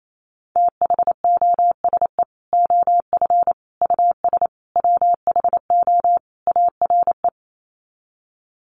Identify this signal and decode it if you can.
Morse code, heard as a keyed tone, about 28 wpm, 700 Hz. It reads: THOSE OF US WHO ARE
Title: Sets of 5 Words Encore2 28wpm